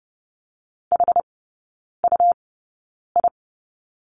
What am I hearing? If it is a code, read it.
Morse code, heard as a keyed tone, about 30 wpm, spaced as for 9 wpm, 700 Hz. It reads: HUI
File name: Lesson 19e - Letter w - Three Characters 30wpm